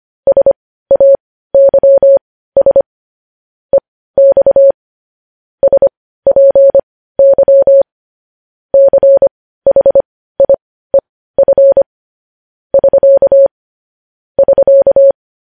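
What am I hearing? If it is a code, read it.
Morse code, heard as a keyed tone, about 25 wpm, spaced as for 15 wpm, 550 Hz. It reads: SAYS EX SPY CHIEF <SK> <SK>
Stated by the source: Morse code news headlines sent at 15.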